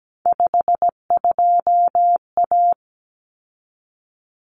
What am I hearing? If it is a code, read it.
Morse code, heard as a keyed tone, about 17 wpm, 700 Hz. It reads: 52A
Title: Common Cut Numbers - 3-Digit 17wpm